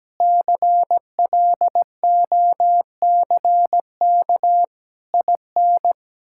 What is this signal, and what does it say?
Morse code, heard as a keyed tone, about 17 wpm, 700 Hz. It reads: CLOCK IN
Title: Pangram Challenge v2 - 17wpm